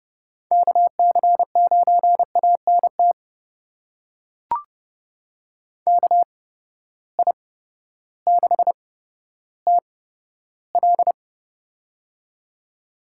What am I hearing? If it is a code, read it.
Morse code, heard as a keyed tone, about 30 wpm, spaced as for 14 wpm, 700 Hz. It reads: KC9ANT KI6TL
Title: USA Call Signs - 8x Character Spacing - 30wpm